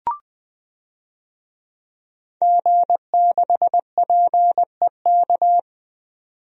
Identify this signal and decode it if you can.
Morse code, heard as a keyed tone, about 20 wpm, 700 Hz. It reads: G6PEK
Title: DXCC-100 Call Signs - 5 Characters - 20wpm